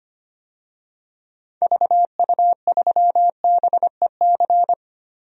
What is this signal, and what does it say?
Morse code, heard as a keyed tone, about 25 wpm, 700 Hz. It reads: VU3BEC
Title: DXCC-100 Call Signs - 6 Characters - Repeated 2x - 25wpm